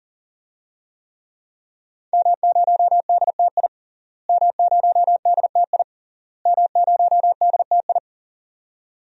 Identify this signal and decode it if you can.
Morse code, heard as a keyed tone, about 37 wpm, 700 Hz. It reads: M0DTI M0DTI M0DTI
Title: DXCC-100 Call Signs - 5 Characters - Repeated 3x - 40wpm